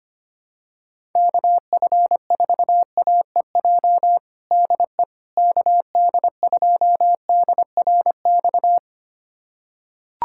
Morse code, heard as a keyed tone, about 25 wpm, 700 Hz. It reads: KF4AEJ DE KD2DRX
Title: USA Callsign DE Callsign - 6 Characters - 25wpm